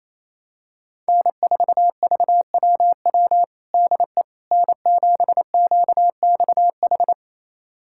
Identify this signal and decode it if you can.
Morse code, heard as a keyed tone, about 28 wpm, 700 Hz. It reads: N4VWW DE N7QXH
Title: USA Callsign DE Callsign - 5 Characters - 28wpm